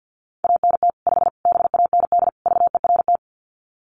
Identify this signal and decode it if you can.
Morse code, heard as a keyed tone, about 50 wpm, 700 Hz. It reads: ANT 5 BAND VERT
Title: Common QSO Phrases 50wpm